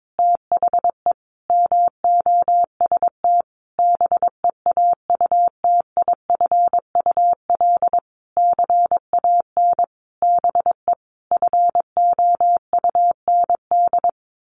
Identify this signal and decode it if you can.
Morse code, heard as a keyed tone, about 22 wpm, 700 Hz. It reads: THE MOST BEAUTIFUL CAN BE FOUND
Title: Sentences from Top 1500 Words 22wpm